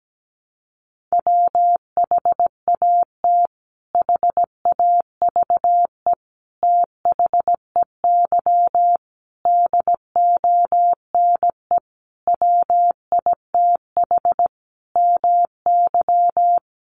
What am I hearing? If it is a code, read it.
Morse code, heard as a keyed tone, about 17 wpm, 700 Hz. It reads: WHAT HAVE THEY DONE WITH MY
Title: Sentences from Top 500 Words 17wpm